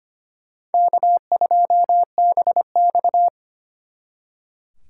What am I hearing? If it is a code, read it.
Morse code, heard as a keyed tone, about 32 wpm, 700 Hz. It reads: K2BX